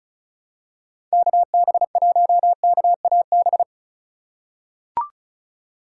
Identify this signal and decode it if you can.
Morse code, heard as a keyed tone, about 34 wpm, 700 Hz. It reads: KB1KAB
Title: USA Call Signs - 6 Characters - 2x Character Spacing - 35wpm